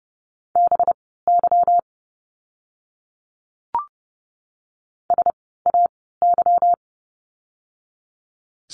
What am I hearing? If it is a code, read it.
Morse code, heard as a keyed tone, about 30 wpm, spaced as for 17 wpm, 700 Hz. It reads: BY SAY
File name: course / 30:10wpm / 230 Lesson 37f - Letter x - Words 30wpm